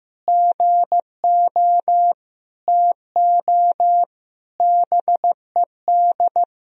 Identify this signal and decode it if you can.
Morse code, heard as a keyed tone, about 16 wpm, 700 Hz. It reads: GO TO BED
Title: Sentences from Top 1000 Words - Review Part 4 - 15wpm